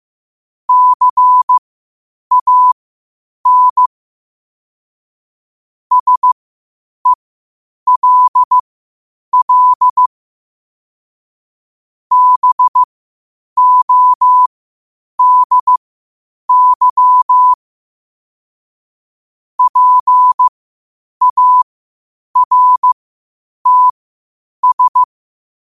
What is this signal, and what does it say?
Morse code, heard as a keyed tone, about 15 wpm, spaced as for 8 wpm, 1000 Hz. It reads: CAN SELL BODY PARTS
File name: qotdfiles / 8wpm